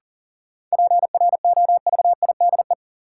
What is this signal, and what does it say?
Morse code, heard as a keyed tone, about 40 wpm, 700 Hz. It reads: PROVIDE
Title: Sentences from Top 1100 Words - New Word Drill Alt - 40wpm